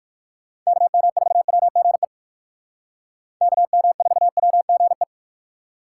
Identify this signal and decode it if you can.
Morse code, heard as a keyed tone, about 58 wpm, 700 Hz. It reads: KM4WGE KM4WGE